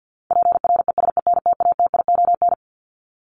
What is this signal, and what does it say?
Morse code, heard as a keyed tone, about 50 wpm, 700 Hz. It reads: PRESENTATION
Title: Top 1501-1750 Words 50wpm